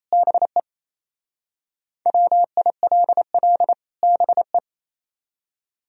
Morse code, heard as a keyed tone, about 36 wpm, 700 Hz. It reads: BE WILL BE